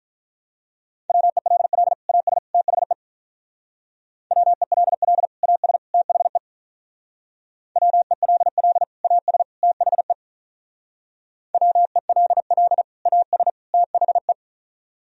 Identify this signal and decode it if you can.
Morse code, heard as a keyed tone, about 45 wpm, 700 Hz. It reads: WELL AS THE WELL AS THE WELL AS THE WELL AS THE